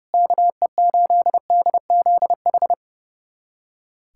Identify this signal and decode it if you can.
Morse code, heard as a keyed tone, about 30 wpm, 700 Hz. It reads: KE8DZH
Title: USA Callsign DE Callsign 30wpm